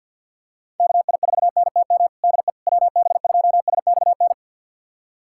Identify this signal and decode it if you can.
Morse code, heard as a keyed tone, about 51 wpm, 700 Hz. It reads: KI4NTM DE WB2SXN
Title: USA Callsign DE Callsign 50wpm